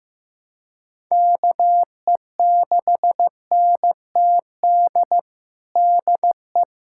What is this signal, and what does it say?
Morse code, heard as a keyed tone, about 16 wpm, 700 Hz. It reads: KE6NTD DE
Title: USA Callsign DE Callsign 15wpm